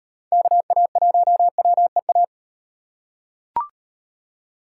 Morse code, heard as a keyed tone, about 37 wpm, 700 Hz. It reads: KA1WEA